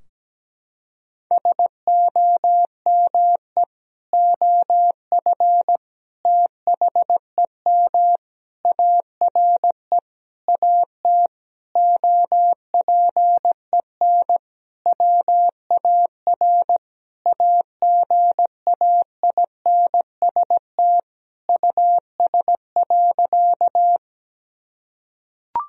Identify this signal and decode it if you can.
Morse code, heard as a keyed tone, about 17 wpm, 700 Hz. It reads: SOME OF THEM ARE AT OPEN WAR AGAINST US.
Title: Sentences from Top 1000 Words - Review Part 2 - 17wpm